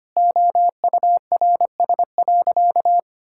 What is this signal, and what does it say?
Morse code, heard as a keyed tone, about 25 wpm, 700 Hz. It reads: OURS.
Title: Sentences from Top 1000 Words - Review Part 5 - 25wpm